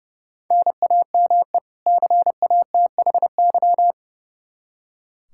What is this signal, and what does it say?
Morse code, heard as a keyed tone, about 30 wpm, 700 Hz. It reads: NAME CATHY
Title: Practice QSO - Exchange 1 - Beginning - 30wpm